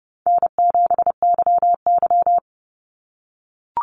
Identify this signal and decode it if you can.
Morse code, heard as a keyed tone, about 30 wpm, 700 Hz. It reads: N7YY
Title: NA SOTA Chaser Callsigns 30wpm